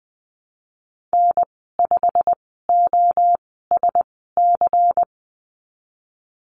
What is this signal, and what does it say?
Morse code, heard as a keyed tone, about 20 wpm, spaced as for 15 wpm, 700 Hz. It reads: N5OSC
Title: 140 Lesson 23g - Letter f - Call Signs 20wpm